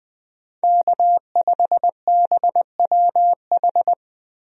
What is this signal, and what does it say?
Morse code, heard as a keyed tone, about 21 wpm, 700 Hz. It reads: K5BWH
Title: USA Call Signs - 5 Characters - Repeated 2x - 20wpm